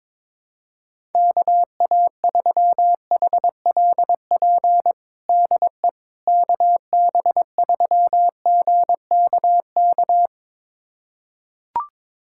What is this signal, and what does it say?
Morse code, heard as a keyed tone, about 21 wpm, 700 Hz. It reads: KA3HLP DE KB3GKK
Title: USA Callsign DE Callsign 22wpm